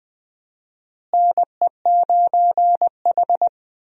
Morse code, heard as a keyed tone, about 20 wpm, 700 Hz. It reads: NE9H
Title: USA Call Signs - 4 Characters - Repeated 3x - 20wpm